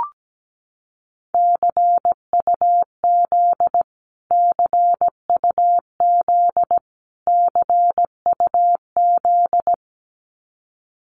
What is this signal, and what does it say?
Morse code, heard as a keyed tone, about 17 wpm, 700 Hz. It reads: CUZ CUZ CUZ
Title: Abbreviations Repeated 3x - 17wpm